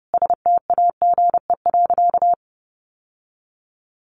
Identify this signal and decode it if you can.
Morse code, heard as a keyed tone, about 30 wpm, 700 Hz. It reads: STAGE.
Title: Sentences from Top 1200 Words 30wpm